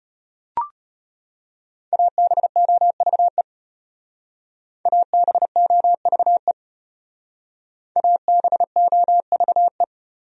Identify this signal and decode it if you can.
Morse code, heard as a keyed tone, about 34 wpm, 700 Hz. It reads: ABOVE ABOVE ABOVE